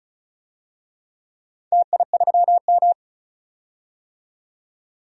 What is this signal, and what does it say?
Morse code, heard as a keyed tone, about 36 wpm, 700 Hz. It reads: TI3M
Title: DXCC-100 Call Signs - 4 Characters - 35wpm